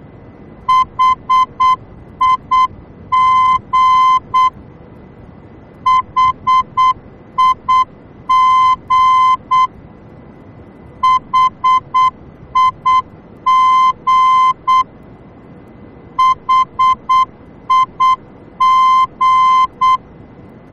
Morse code, heard as a keyed tone, about 8 wpm, 1030 Hz. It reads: HIG HIG HIG HIG
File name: germany / mx / HIG-D